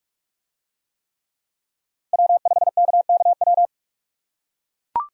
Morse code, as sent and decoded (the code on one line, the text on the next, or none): .-- ..... -.- -.- .--
W5KKW